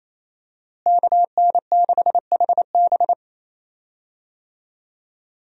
Morse code, heard as a keyed tone, about 27 wpm, 700 Hz. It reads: KN6HB
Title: USA Call Signs - 5 Characters - Repeated 3x - 28wpm